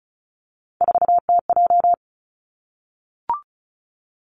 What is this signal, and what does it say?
Morse code, heard as a keyed tone, about 34 wpm, 700 Hz. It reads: 4TJ
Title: Three Challenging Letter-Number Combinations 35wpm